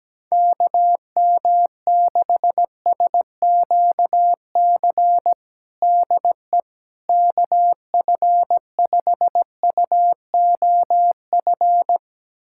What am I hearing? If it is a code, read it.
Morse code, heard as a keyed tone, about 17 wpm, 700 Hz. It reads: KM6SQC DE KF5UOF
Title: USA Callsign DE Callsign - 6 Characters - 17wpm